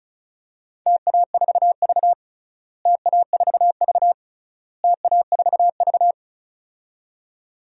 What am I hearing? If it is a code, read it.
Morse code, heard as a keyed tone, about 36 wpm, 700 Hz. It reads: TA4V TA4V TA4V
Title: DXCC-100 Call Signs - 4 Characters - Repeated 3x - 35wpm